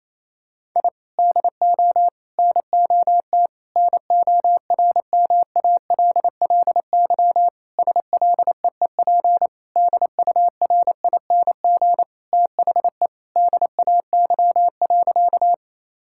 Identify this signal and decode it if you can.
Morse code, heard as a keyed tone, about 28 wpm, 700 Hz. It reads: I DO NOT NORMALLY SLEEP DURING THE DAY.
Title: Sentences from Top 1000 Words - Review Part 6 - 28wpm